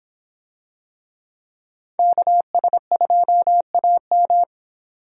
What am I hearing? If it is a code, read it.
Morse code, heard as a keyed tone, about 26 wpm, 700 Hz. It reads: KS2AM